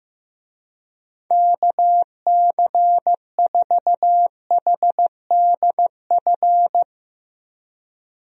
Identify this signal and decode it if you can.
Morse code, heard as a keyed tone, about 15 wpm, 700 Hz. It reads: KC4HDF
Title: USA Call Signs - 6 Characters - Repeated 3x - 15wpm